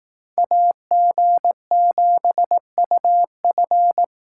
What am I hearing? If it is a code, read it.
Morse code, heard as a keyed tone, about 18 wpm, 700 Hz. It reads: AG7UF
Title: Speed Racing - USA 2x2 Call Signs 15wpm